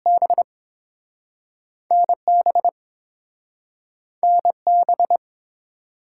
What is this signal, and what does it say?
Morse code, heard as a keyed tone, about 30 wpm, 700 Hz. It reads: B NB NB